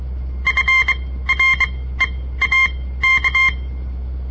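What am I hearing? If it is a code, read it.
Morse code, heard as a keyed tone, about 23 wpm, spaced as for 15 wpm, 2100 Hz. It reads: FREAK
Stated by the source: On-air Morse code practice broadcast by a southern Montana amateur radio association; all code, no read back.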